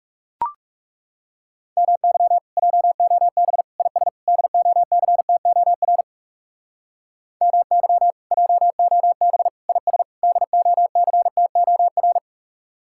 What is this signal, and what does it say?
Morse code, heard as a keyed tone, about 38 wpm, 700 Hz. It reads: MY JOB IS DOCTOR MY JOB IS DOCTOR
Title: Speed-Racing Common QSO Phrases - 30wpm